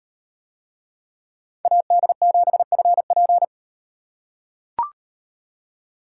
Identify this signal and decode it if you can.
Morse code, heard as a keyed tone, about 38 wpm, 700 Hz. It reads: AD7FP